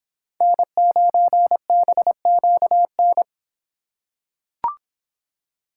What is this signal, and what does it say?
Morse code, heard as a keyed tone, about 24 wpm, 700 Hz. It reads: N9BQN